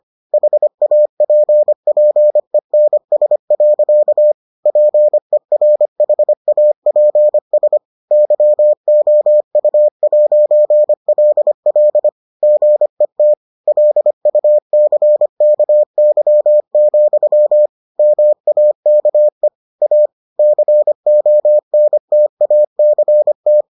Morse code, heard as a keyed tone, about 25 wpm, 600 Hz. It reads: HAPPENS. PERHAPS YOU'LL GET LUCKY, MAKE A CONTACT